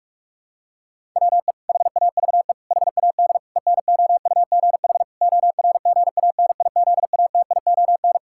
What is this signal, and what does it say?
Morse code, heard as a keyed tone, about 46 wpm, 700 Hz. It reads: WE HAVE HAD ENOUGH ORGANIZATION
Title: Sentences from Top 1000 Words - Review Part 5 - 45wpm